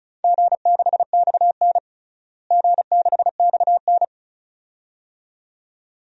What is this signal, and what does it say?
Morse code, heard as a keyed tone, about 36 wpm, 700 Hz. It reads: G6XN G6XN
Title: DXCC-100 Call Signs - 4 Characters - Repeated 2x - 35wpm